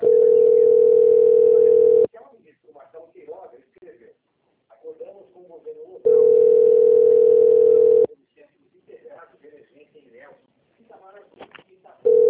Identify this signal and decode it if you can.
Ringback tone (North America), 440 + 480 Hz, with speech.
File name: rg-1600-unknown- - - .10378